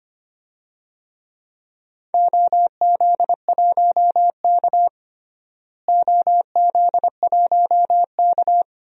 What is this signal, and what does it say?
Morse code, heard as a keyed tone, about 25 wpm, 700 Hz. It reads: OZ1K OZ1K
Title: DXCC-100 Call Signs - 4 Characters - Repeated 3x - 25wpm